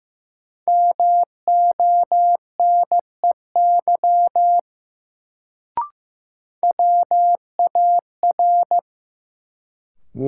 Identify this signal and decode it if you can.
Morse code, heard as a keyed tone, about 15 wpm, 700 Hz. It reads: MONEY WAR
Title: Sentences from Top 400 Words - New Word Drill - 15wpm